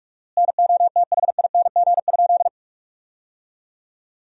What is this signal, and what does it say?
Morse code, heard as a keyed tone, about 42 wpm, 700 Hz. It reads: NOTHING?
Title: 4x Word-Spacing - Sentences from Top 500 Words - 45wpm